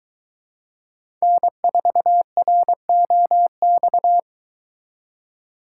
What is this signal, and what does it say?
Morse code, heard as a keyed tone, about 22 wpm, 700 Hz. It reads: N4ROX